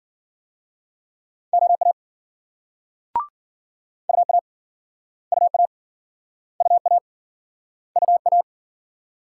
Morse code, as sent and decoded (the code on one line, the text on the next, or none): --- .- / ..- .- / ..- .- / ..- .- / ..- .-
OA UA UA UA UA